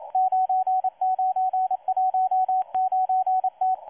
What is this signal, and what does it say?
Morse code, heard as a keyed tone, about 27 wpm, 750 Hz, with radio noise. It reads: 9919T
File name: M12-2016-07-07-1310utc-13926khz-error-restart